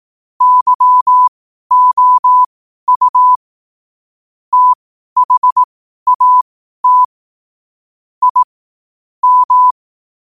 Morse code, heard as a keyed tone, about 18 wpm, spaced as for 12 wpm, 1000 Hz. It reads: YOU THAT I M